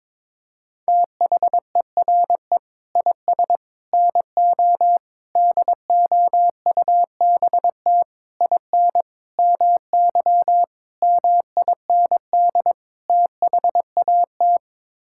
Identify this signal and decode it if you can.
Morse code, heard as a keyed tone, about 22 wpm, 700 Hz. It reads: THERE IS NO DOUBT IN MY MIND THAT
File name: practice / 22wpm / Sets of 8 Words - 22wpm